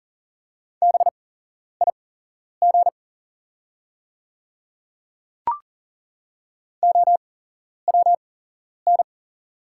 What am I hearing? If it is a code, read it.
Morse code, heard as a keyed tone, about 40 wpm, spaced as for 11 wpm, 700 Hz. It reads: BIG OWN
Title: 8x Character Spacing - Top 500 Words - 40wpm